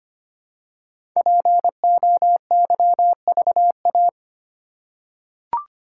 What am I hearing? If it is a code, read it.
Morse code, heard as a keyed tone, about 25 wpm, 700 Hz. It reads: POYVA
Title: Code Groups - 5 Letters - 25wpm